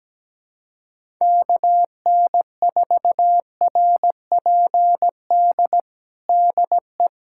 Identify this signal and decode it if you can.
Morse code, heard as a keyed tone, about 17 wpm, 700 Hz. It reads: KN4RPD DE
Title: USA Callsign DE Callsign - 6 Characters - 17wpm